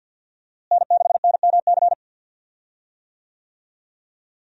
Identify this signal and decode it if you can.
Morse code, heard as a keyed tone, about 48 wpm, 700 Hz. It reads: N6NMC
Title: USA Call Signs - 5 Characters - Repeated 2x - 50wpm